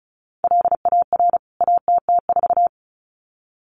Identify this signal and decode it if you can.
Morse code, heard as a keyed tone, about 34 wpm, 700 Hz. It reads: LAR ATT4
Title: CWT Exchange 35wpm